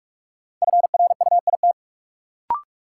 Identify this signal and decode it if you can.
Morse code, heard as a keyed tone, about 46 wpm, 700 Hz. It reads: FRUIT
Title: Sentences from Top 1300 Words - New Word Drill Alt - 45wpm